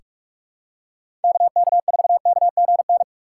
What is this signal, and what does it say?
Morse code, heard as a keyed tone, about 46 wpm, 700 Hz. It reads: KK4KGN
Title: USA Call Signs - 6 Characters - 3x Character Spacing - 45wpm